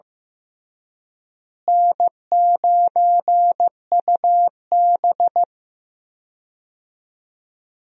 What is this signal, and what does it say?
Morse code, heard as a keyed tone, about 15 wpm, 700 Hz. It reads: N9UB
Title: USA Call Signs - 4 Characters - Repeated 3x - 15wpm